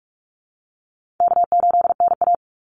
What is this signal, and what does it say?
Morse code, heard as a keyed tone, about 44 wpm, 700 Hz. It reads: K8NA
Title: Easy Call Signs - 45wpm